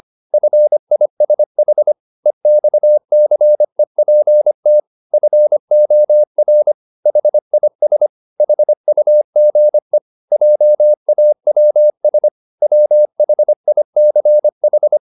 Morse code, heard as a keyed tone, about 25 wpm, 600 Hz. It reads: FISH EXCEPT FOR HIS HUGE JAWS WHICH